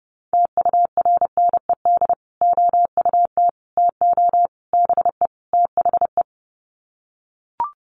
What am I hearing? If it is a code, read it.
Morse code, heard as a keyed tone, about 30 wpm, 700 Hz. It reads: TURNED OUT TO BE THE
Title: Sets of 5 Words Encore 30wpm